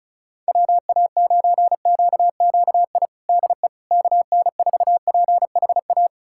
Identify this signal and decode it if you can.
Morse code, heard as a keyed tone, about 35 wpm, 700 Hz. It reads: WA9QQI DE KN4PHA
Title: USA Callsign DE Callsign 35wpm